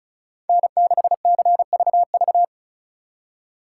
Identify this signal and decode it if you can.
Morse code, heard as a keyed tone, about 33 wpm, 700 Hz. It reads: N6CVV
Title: Speed Racing - USA Call Signs 35wpm